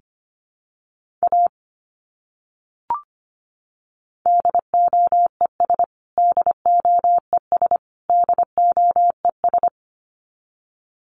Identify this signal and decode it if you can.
Morse code, heard as a keyed tone, about 25 wpm, 700 Hz. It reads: A DOES DOES DOES
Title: Top 200 Words Repeated 25wpm